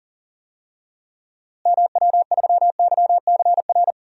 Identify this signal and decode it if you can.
Morse code, heard as a keyed tone, about 40 wpm, 700 Hz. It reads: MW3YCR
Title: DXCC-100 Call Signs - 6 Characters - Repeated 3x - 40wpm